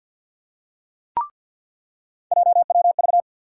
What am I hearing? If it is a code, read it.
Morse code, heard as a keyed tone, about 50 wpm, 700 Hz. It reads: JWV
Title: USA Call Sign Postfixes - 3 Letters 50wpm